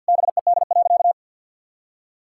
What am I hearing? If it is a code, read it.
Morse code, heard as a keyed tone, about 52 wpm, 700 Hz. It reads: BED.
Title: Sentences from Top 1000 Words - Review Part 6 - 50wpm